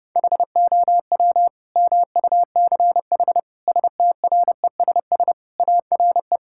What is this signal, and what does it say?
Morse code, heard as a keyed tone, about 30 wpm, 700 Hz. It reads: HOW MUCH STRESS ARE
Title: Sentences from Top 1000 Words - Review Part 6 - 30wpm